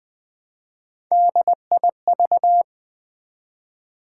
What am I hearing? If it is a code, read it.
Morse code, heard as a keyed tone, about 20 wpm, 700 Hz. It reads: DIV
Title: Build A Word - 3 Letters At A Time - 9-Letter Words - 20wpm